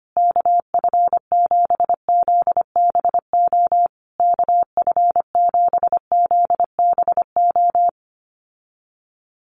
Morse code, heard as a keyed tone, about 25 wpm, 700 Hz. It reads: KF7ZBO KF7ZBO
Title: USA Call Signs Repeated 3x 25wpm